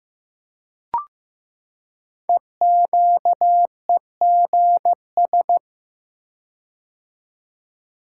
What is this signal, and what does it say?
Morse code, heard as a keyed tone, about 15 wpm, 700 Hz. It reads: EQEGS
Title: Code Groups - 5 Letters - 15wpm